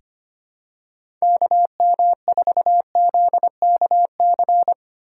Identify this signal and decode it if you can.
Morse code, heard as a keyed tone, about 25 wpm, 700 Hz. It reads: KM4ZKC
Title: USA Call Signs - 6 Characters - 2x Character Spacing - 25wpm